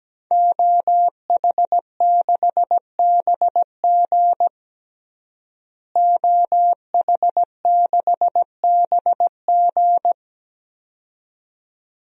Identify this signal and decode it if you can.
Morse code, heard as a keyed tone, about 17 wpm, 700 Hz. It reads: OH6BG OH6BG
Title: DXCC-100 Call Signs - 5 Characters - Repeated 2x - 17wpm